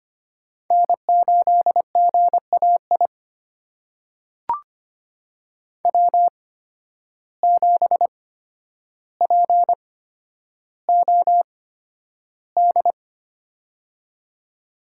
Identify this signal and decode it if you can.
Morse code, heard as a keyed tone, about 25 wpm, spaced as for 11 wpm, 700 Hz. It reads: N8GAI W7POD
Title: USA Call Signs - 5 Characters - 8x Character Spacing - 25wpm